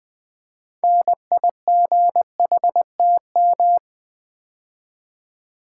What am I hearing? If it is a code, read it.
Morse code, heard as a keyed tone, about 20 wpm, 700 Hz. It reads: NIGHTM
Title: Build A Word - 3 Letters At A Time - 9-Letter Words - 20wpm